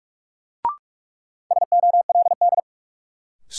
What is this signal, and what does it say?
Morse code, heard as a keyed tone, about 41 wpm, 700 Hz. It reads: SOLD
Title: Sentences from Top 1400 Words - New Word Drill - 45wpm